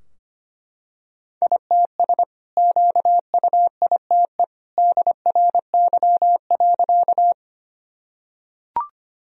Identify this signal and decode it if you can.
Morse code, heard as a keyed tone, about 25 wpm, 700 Hz. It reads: ITS QUITE DRY.
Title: Sentences from Top 1000 Words - Review Part 3 - 25wpm